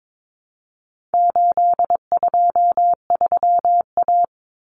Morse code, heard as a keyed tone, about 22 wpm, 700 Hz. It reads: 823A
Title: Common Cut Numbers - 4-Digit 22wpm